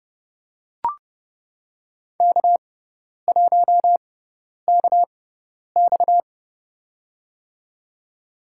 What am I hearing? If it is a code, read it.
Morse code, heard as a keyed tone, about 31 wpm, spaced as for 10 wpm, 700 Hz. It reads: K1KX
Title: USA Call Signs - 6x Character Spacing - 30wpm